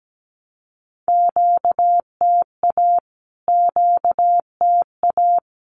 Morse code, heard as a keyed tone, about 17 wpm, 700 Hz. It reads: QTA QTA
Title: Q-Codes Repeated 3x 17wpm